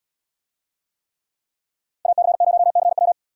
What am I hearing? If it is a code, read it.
Morse code, heard as a keyed tone, about 71 wpm, 700 Hz. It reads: AC0CW